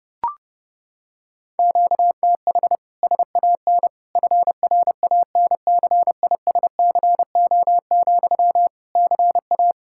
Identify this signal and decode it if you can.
Morse code, heard as a keyed tone, about 32 wpm, 700 Hz. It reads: QTH SAN FRANCISCO, CA
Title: K1USN SST QSO Practice 30wpm